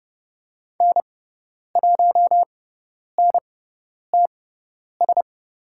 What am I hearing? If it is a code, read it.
Morse code, heard as a keyed tone, about 30 wpm, spaced as for 10 wpm, 700 Hz. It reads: N1NTS
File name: Lesson 18g - Letter m - Call Signs 30wpm